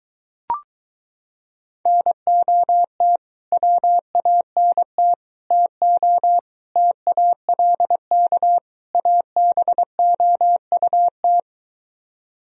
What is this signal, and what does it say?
Morse code, heard as a keyed tone, about 22 wpm, 700 Hz. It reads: NOT WANT TO TALK ABOUT